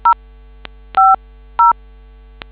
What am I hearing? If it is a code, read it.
DTMF, heard as telephone touch tones, tones of about 150 ms. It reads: 050